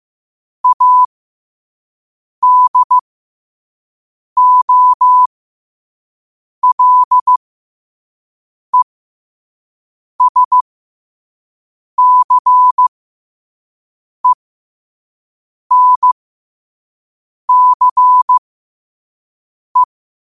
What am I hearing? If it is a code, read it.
Morse code, heard as a keyed tone, about 15 wpm, spaced as for 5 wpm, 1000 Hz. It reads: ADOLESCENCE
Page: New quotes every day in morse code at 5 Words per minute.